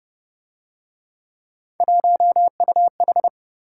Morse code, heard as a keyed tone, about 31 wpm, 700 Hz. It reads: 1UH
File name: Speed-Racing - Three Letter-Number Combinations - 25wpm